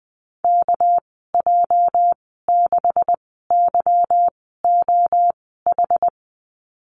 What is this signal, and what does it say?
Morse code, heard as a keyed tone, about 20 wpm, spaced as for 15 wpm, 700 Hz. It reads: KJ6YOH
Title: 238 Lesson 38g - Letter q - Call Signs 20wpm